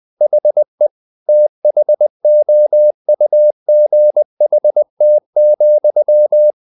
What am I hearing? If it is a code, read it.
Morse code, heard as a keyed tone, about 20 wpm, 600 Hz. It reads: HE THOUGHT,
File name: the old man and the sea part 2 20wpm